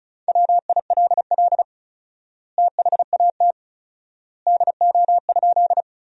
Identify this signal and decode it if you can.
Morse code, heard as a keyed tone, about 36 wpm, 700 Hz. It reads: WILL THAT DO?
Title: 4x Word-Spacing - Sentences from Top 100 Words - 35wpm